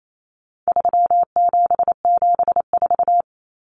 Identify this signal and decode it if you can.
Morse code, heard as a keyed tone, about 28 wpm, 700 Hz. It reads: 3774
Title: Common Cut Numbers - 4-Digit 28wpm